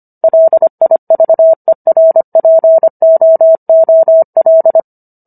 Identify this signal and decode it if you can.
Morse code, heard as a keyed tone, about 25 wpm, 650 Hz. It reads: LIVERPOOL